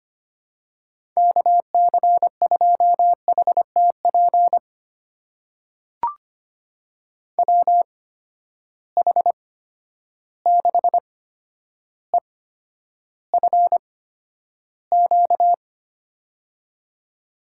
Morse code, heard as a keyed tone, about 25 wpm, spaced as for 11 wpm, 700 Hz. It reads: KC2HTP WH6EFQ
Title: USA Call Signs - 6 Characters - 8x Character Spacing - 25wpm